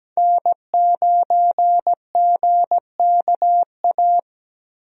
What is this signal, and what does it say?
Morse code, heard as a keyed tone, about 17 wpm, 700 Hz. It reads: N9GKA
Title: USA Call Signs - 5 Characters - 17wpm